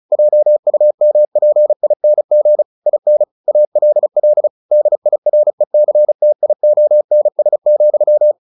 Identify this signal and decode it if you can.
Morse code, heard as a keyed tone, about 34 wpm, 600 Hz. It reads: JUMPING IN ALL DIRECTIONS,
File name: the old man and the sea / the old man and the sea 1 35wpm